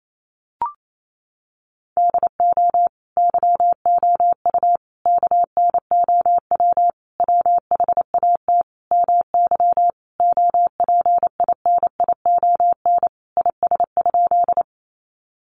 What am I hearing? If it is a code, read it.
Morse code, heard as a keyed tone, about 28 wpm, 700 Hz. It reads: DO YOU KNOW WHAT MY OPINION IS?
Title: Sentences from Top 1200 Words 28wpm